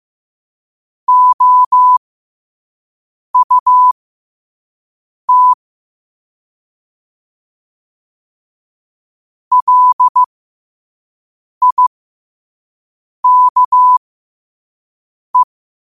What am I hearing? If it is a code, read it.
Morse code, heard as a keyed tone, about 15 wpm, spaced as for 5 wpm, 1000 Hz. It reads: OUT LIKE
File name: qotdfiles / 5wpm